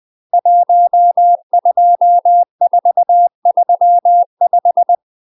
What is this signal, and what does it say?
Morse code, heard as a keyed tone, about 20 wpm, 700 Hz. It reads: 12435
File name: UK MASTER QSOs TRAINING / 20WPM / UK MASTER QSOs 0029